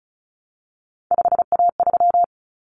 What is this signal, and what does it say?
Morse code, heard as a keyed tone, about 34 wpm, 700 Hz. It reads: 5A3
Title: Common Cut Numbers - 3-Digit 35wpm